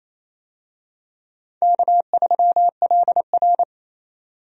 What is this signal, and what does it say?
Morse code, heard as a keyed tone, about 28 wpm, 700 Hz. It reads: K3LR
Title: USA Call Signs - 4 Characters - Repeated 3x - 28wpm